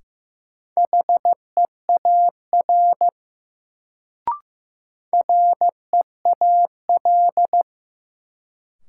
Morse code, heard as a keyed tone, about 15 wpm, 700 Hz. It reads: HEAR REAL
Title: Sentences from Top 300 Words - New Word Drill - 15wpm